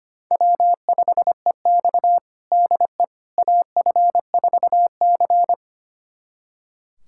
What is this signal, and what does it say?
Morse code, heard as a keyed tone, about 25 wpm, 700 Hz. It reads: W5EX DE AF4C
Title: USA Callsign DE Callsign - 4 Characters - 25wpm